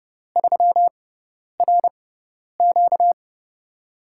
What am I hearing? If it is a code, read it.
Morse code, heard as a keyed tone, about 30 wpm, spaced as for 10 wpm, 700 Hz. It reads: 3RQ
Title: USA Call Signs - 5 Characters - 6x Character Spacing - 30wpm